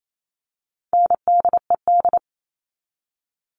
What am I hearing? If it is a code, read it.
Morse code, heard as a keyed tone, about 28 wpm, 700 Hz. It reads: NDED
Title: All Cut Numbers - 4-Digit 28wpm